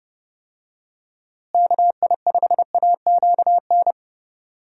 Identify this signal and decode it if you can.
Morse code, heard as a keyed tone, about 30 wpm, 700 Hz. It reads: KI5AQN
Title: USA Call Signs - 6 Characters - Repeated 2x - 30wpm